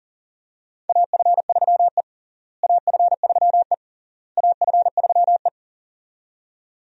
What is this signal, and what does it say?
Morse code, heard as a keyed tone, about 40 wpm, 700 Hz. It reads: AF3E AF3E AF3E
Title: USA Call Signs - 4 Characters - Repeated 3x - 40wpm